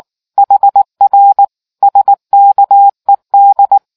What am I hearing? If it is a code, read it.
Morse code, heard as a keyed tone, about 19 wpm, 800 Hz. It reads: HR SKED